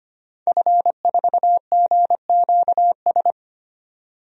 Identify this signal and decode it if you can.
Morse code, heard as a keyed tone, about 25 wpm, 700 Hz. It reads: F4GQS
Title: DXCC-100 Call Signs - 5 Characters - 25wpm